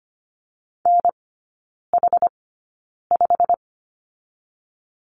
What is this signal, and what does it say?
Morse code, heard as a keyed tone, about 25 wpm, spaced as for 9 wpm, 690 Hz. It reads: NH5
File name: Lesson 15e - Number 5 - Three Characters 25wpm